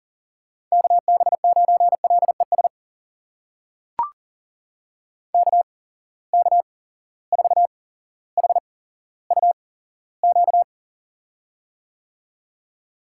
Morse code, heard as a keyed tone, about 39 wpm, spaced as for 17 wpm, 700 Hz. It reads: KB9LES KK4HUQ
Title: USA Call Signs - 6 Characters - 8x Character Spacing - 40wpm